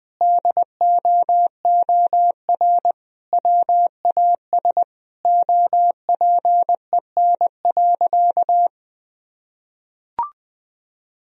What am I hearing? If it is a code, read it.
Morse code, heard as a keyed tone, about 20 wpm, 700 Hz. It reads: DOOR WAS OPEN.
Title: Sentences from Top 1000 Words - Review Part 2 - 20wpm